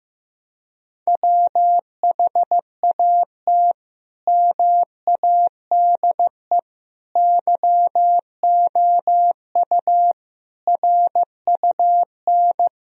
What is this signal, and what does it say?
Morse code, heard as a keyed tone, about 15 wpm, 700 Hz. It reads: WHAT MADE YOU RUN
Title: Sentences from Top 1000 Words - Review Part 4 - 15wpm